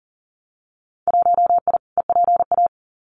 Morse code, heard as a keyed tone, about 40 wpm, 700 Hz. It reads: 1I EPA
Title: Winter Field Day Exchanges Encore 40wpm